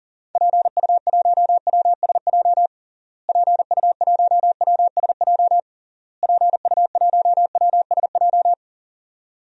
Morse code, heard as a keyed tone, about 37 wpm, 700 Hz. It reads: PU1WSJ PU1WSJ PU1WSJ
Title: DXCC-100 Call Signs - 6 Characters - Repeated 3x - 40wpm